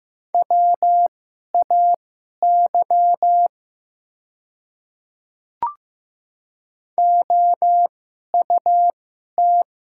Morse code, heard as a keyed tone, about 15 wpm, spaced as for 11 wpm, 700 Hz. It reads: WAY OUT
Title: 2x Character Spacing - Top 100 Words - 15wpm